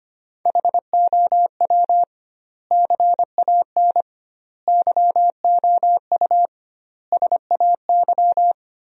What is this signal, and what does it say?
Morse code, heard as a keyed tone, about 25 wpm, 700 Hz. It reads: HOW CAN YOU SAY
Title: 2x Word-Spacing - Sentences from Top 200 Words - 25wpm